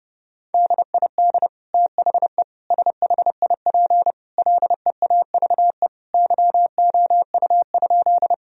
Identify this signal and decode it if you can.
Morse code, heard as a keyed tone, about 30 wpm, 700 Hz. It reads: DID THE SHIP LEAVE YOU?
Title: Sentences from Top 1000 Words - Review Part 2 - 30wpm